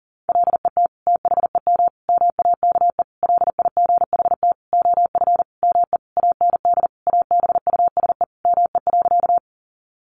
Morse code, heard as a keyed tone, about 40 wpm, 700 Hz. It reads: LET THEM MAKE LIGHT OF ME AND ABUSE ME.
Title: Sentences from Top 1300 Words 40wpm